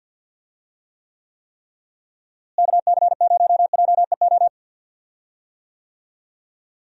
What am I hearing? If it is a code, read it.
Morse code, heard as a keyed tone, about 49 wpm, 700 Hz. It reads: KC0JEO
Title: USA Call Signs - 6 Characters - Repeated 3x - 50wpm